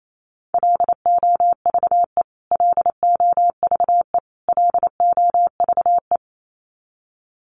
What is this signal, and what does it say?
Morse code, heard as a keyed tone, about 28 wpm, 700 Hz. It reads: LOVE LOVE LOVE
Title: Top 500 Words Repeated 28wpm